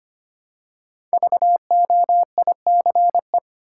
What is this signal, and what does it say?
Morse code, heard as a keyed tone, about 25 wpm, 700 Hz. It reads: VOICE
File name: 5 Letter Words - First 3 Letters Given - 25wpm